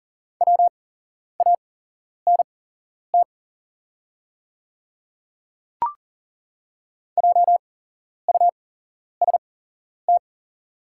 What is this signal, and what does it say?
Morse code, heard as a keyed tone, about 40 wpm, spaced as for 11 wpm, 700 Hz. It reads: WANT JUST
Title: 8x Character Spacing - Top 300 Words - 40wpm